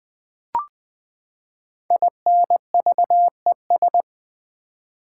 Morse code, heard as a keyed tone, about 20 wpm, 700 Hz. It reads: INVES
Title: Build A Word - 5 Letters At A Time - 12-Letter Words - 20wpm